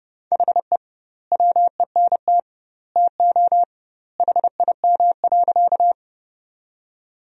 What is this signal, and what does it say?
Morse code, heard as a keyed tone, about 30 wpm, 700 Hz. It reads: HE WENT TO HIM.
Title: 2x Word-Spacing - Sentences from Top 200 Words - 30wpm